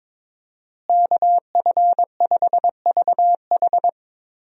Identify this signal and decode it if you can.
Morse code, heard as a keyed tone, about 22 wpm, 700 Hz. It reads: KF5VH
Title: USA Call Signs - 5 Characters - Repeated 3x - 22wpm